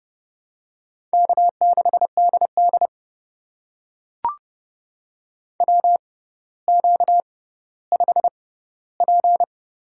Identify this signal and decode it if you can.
Morse code, heard as a keyed tone, about 30 wpm, spaced as for 15 wpm, 700 Hz. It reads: K6DD WQ5P
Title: USA Call Signs - 4 Characters - 6x Character Spacing - 30wpm